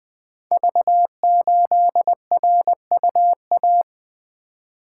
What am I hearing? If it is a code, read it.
Morse code, heard as a keyed tone, about 20 wpm, 700 Hz. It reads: V8RUA
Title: Code Groups - 5 Letter:Numbers - 20wpm